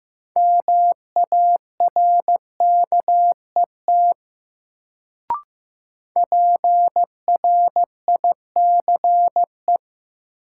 Morse code, heard as a keyed tone, about 15 wpm, 700 Hz. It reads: MARKET PRICE
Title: Sentences from Top 600 Words - New Word Drill - 15wpm